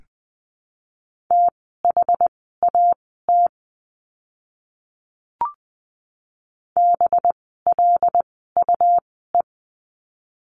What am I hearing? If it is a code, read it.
Morse code, heard as a keyed tone, about 20 wpm, spaced as for 15 wpm, 700 Hz. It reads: THAT BLUE
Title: 185 Lesson 30f - Letter b - Words 20wpm